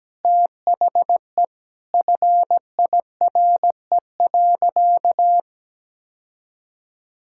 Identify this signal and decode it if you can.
Morse code, heard as a keyed tone, about 17 wpm, 700 Hz. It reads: THE FIRE.
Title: Sentences from Top 1600 Words 17wpm